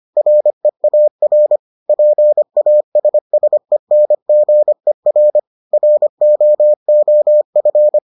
Morse code, heard as a keyed tone, about 25 wpm, 600 Hz. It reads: REAR PASSENGER ROOF